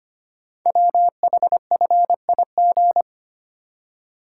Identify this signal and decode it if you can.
Morse code, heard as a keyed tone, about 25 wpm, 700 Hz. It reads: WHFIG
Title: Code Groups - 5 Letters - 25wpm